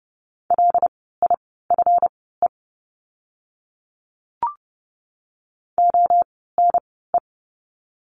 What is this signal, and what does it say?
Morse code, heard as a keyed tone, about 30 wpm, spaced as for 17 wpm, 700 Hz. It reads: LIFE ONE
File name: course / 30:10wpm / 230 Lesson 37f - Letter x - Words 30wpm